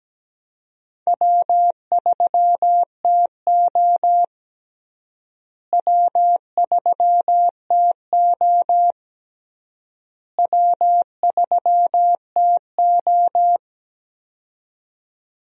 Morse code, heard as a keyed tone, about 17 wpm, 700 Hz. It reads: W3TO W3TO W3TO
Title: USA Call Signs - 4 Characters - Repeated 3x - 17wpm